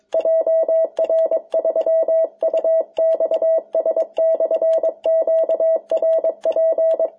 Morse code, heard as a keyed tone, about 22 wpm, 650 Hz. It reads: JR3UXS/QRP
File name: jh8cbh / 2021cw